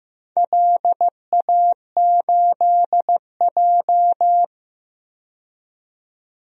Morse code, heard as a keyed tone, about 15 wpm, 700 Hz. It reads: LA8J
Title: DXCC-100 Call Signs - 4 Characters - 15wpm